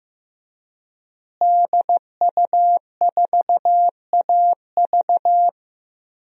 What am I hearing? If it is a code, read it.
Morse code, heard as a keyed tone, about 15 wpm, 700 Hz. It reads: DU4AV
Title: DXCC-100 Call Signs - 5 Characters - Repeated 2x - 15wpm